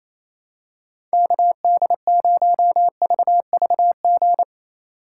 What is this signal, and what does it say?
Morse code, heard as a keyed tone, about 28 wpm, 700 Hz. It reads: KD0VVG
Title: USA Call Signs - 2x Character Spacing - 28wpm